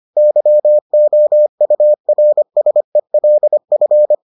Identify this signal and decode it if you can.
Morse code, heard as a keyed tone, about 25 wpm, 600 Hz. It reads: YOURSELF